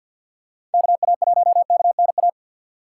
Morse code, heard as a keyed tone, about 53 wpm, 700 Hz. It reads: KA1KNA
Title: USA Call Signs - 6 Characters - 4x Character Spacing - 50wpm